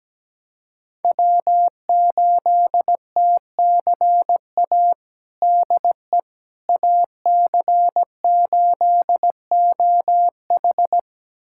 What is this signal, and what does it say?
Morse code, heard as a keyed tone, about 17 wpm, 700 Hz. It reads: W8TCA DE AC8OH
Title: USA Callsign DE Callsign - 5 Characters - 17wpm